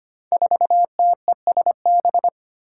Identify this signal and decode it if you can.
Morse code, heard as a keyed tone, about 25 wpm, 700 Hz. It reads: 4TESB
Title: Five Challenging Letter-Number Combinations 25wpm